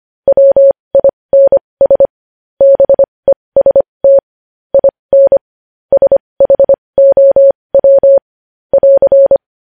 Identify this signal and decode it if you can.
Morse code, heard as a keyed tone, about 25 wpm, spaced as for 20 wpm, 550 Hz. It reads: WINS BEST IN SHOW <AR>
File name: podcast / News headlines-1x-25x20